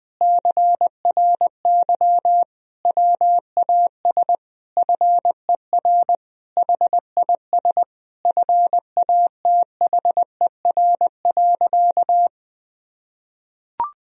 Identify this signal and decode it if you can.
Morse code, heard as a keyed tone, about 20 wpm, 700 Hz. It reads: CRY WAS FER HIS FATHER.
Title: Sentences from Top 500 Words and Abbreviations 20wpm